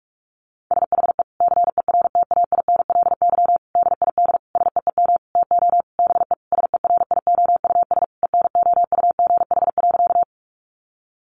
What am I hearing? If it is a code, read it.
Morse code, heard as a keyed tone, about 44 wpm, 700 Hz. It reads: SHE CERTAINLY DID SEEM TO BE SERIOUS ENOUGH.
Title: Sentences from Top 1100 Words 45wpm